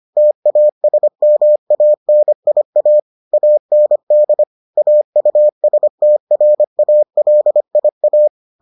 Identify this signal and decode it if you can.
Morse code, heard as a keyed tone, about 25 wpm, 600 Hz. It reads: TASMANIA AND AUSTRALIA